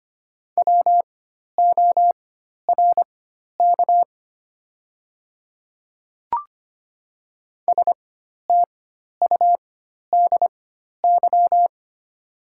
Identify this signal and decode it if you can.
Morse code, heard as a keyed tone, about 25 wpm, spaced as for 12 wpm, 700 Hz. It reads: WORK STUDY
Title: 4x Character Spacing - Top 300 Words - 25wpm